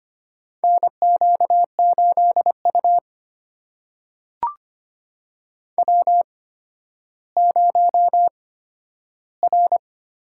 Morse code, heard as a keyed tone, about 25 wpm, spaced as for 12 wpm, 700 Hz. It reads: NQ8U W0R
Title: USA Call Signs - 8x Character Spacing - 25wpm